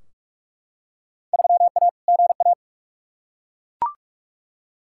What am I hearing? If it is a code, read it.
Morse code, heard as a keyed tone, about 41 wpm, 700 Hz. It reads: 3A GA
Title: Field Day Exchanges Encore - 45wpm